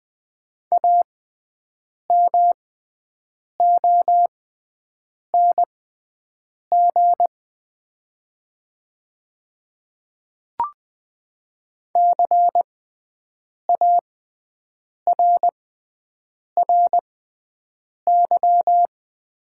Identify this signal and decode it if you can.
Morse code, heard as a keyed tone, about 20 wpm, spaced as for 7 wpm, 700 Hz. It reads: AMONG CARRY
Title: 6x Character Spacing - Top 500 Words - 20wpm